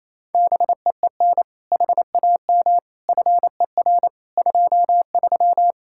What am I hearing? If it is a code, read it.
Morse code, heard as a keyed tone, about 28 wpm, 700 Hz. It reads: BEEN HAM FER 23
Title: Practice QSO - Exchange 3 - More Info - 28wpm